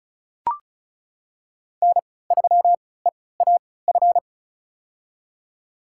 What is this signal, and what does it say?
Morse code, heard as a keyed tone, about 36 wpm, spaced as for 20 wpm, 700 Hz. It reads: N3EAF
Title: USA Call Signs - 5 Characters - 3x Character Spacing - 35wpm